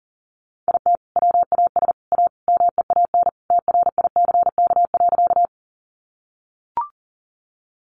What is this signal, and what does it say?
Morse code, heard as a keyed tone, about 37 wpm, 700 Hz. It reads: IT WAS A MEAN TRICK.
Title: Sentences from Top 1500 Words 40wpm